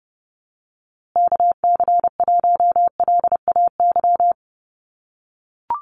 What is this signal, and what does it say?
Morse code, heard as a keyed tone, about 30 wpm, 700 Hz. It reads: KC1LAY
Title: USA Call Signs Repeated 2x 30wpm